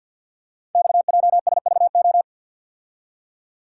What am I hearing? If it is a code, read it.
Morse code, heard as a keyed tone, about 52 wpm, 700 Hz. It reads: XJSVO
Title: Code Groups - 5 Letters - 50wpm